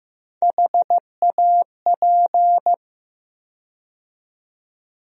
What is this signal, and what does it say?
Morse code, heard as a keyed tone, about 15 wpm, 700 Hz. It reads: HAP
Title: USA Call Sign Postfixes - 3 Letters 15wpm